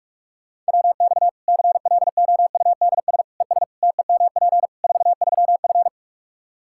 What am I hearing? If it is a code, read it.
Morse code, heard as a keyed tone, about 46 wpm, 700 Hz. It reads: WX CLOUDS ES TEMP 43F
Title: Practice QSO - Exchange 2 - Info - 45wpm